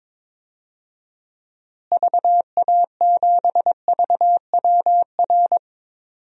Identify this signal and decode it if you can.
Morse code, heard as a keyed tone, about 22 wpm, 700 Hz. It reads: VA7VWR
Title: DXCC-100 Call Signs - 6 Characters - Repeated 2x - 22wpm